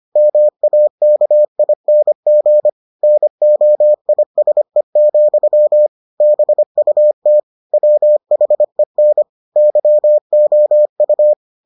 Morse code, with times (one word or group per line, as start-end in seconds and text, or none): 0.2-2.7 MAKING
3.0-5.9 NOISE,
6.2-7.4 BUT
7.7-9.2 WHEN
9.6-11.3 YOU